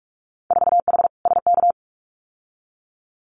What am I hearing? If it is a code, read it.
Morse code, heard as a keyed tone, about 46 wpm, 690 Hz. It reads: 4H SK
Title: Winter Field Day Exchanges Encore 45wpm